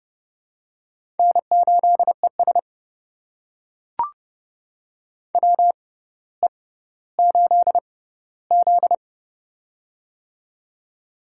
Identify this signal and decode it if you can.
Morse code, heard as a keyed tone, about 30 wpm, spaced as for 15 wpm, 700 Hz. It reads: N8ES WE8Z
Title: USA Call Signs - 4 Characters - 6x Character Spacing - 30wpm